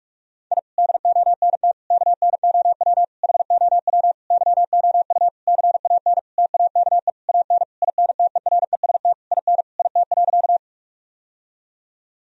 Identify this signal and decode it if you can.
Morse code, heard as a keyed tone, about 41 wpm, 700 Hz. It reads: I DONT KNOW HOW YOU CAN TAKE AN INTEREST IN IT.
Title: Sentences from Top 1000 Words - Review Part 2 - 45wpm